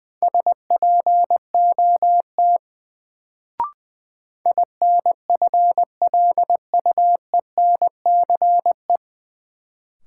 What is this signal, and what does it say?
Morse code, heard as a keyed tone, about 20 wpm, 700 Hz. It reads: SPOT INFLUENCE
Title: Sentences from Top 1100 Words - New Word Drill Alt - 20wpm